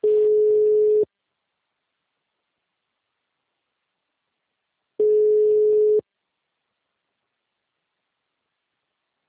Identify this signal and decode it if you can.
Ringback tone (Europe, 425 Hz).